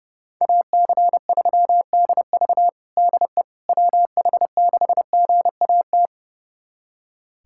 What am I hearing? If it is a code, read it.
Morse code, heard as a keyed tone, about 30 wpm, 700 Hz. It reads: AC3DV DE WH6GAT
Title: USA Callsign DE Callsign 30wpm